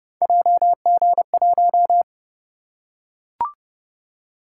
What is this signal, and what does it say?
Morse code, heard as a keyed tone, about 28 wpm, 700 Hz. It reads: JG1